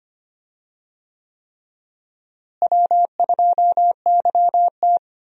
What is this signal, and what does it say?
Morse code, heard as a keyed tone, about 25 wpm, 700 Hz. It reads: W2YT
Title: USA Call Signs - 4 Characters - Repeated 3x - 25wpm